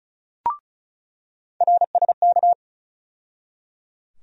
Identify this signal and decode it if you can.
Morse code, heard as a keyed tone, about 34 wpm, 700 Hz. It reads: RSK
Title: USA Call Sign Postfixes - 3 Letters 35wpm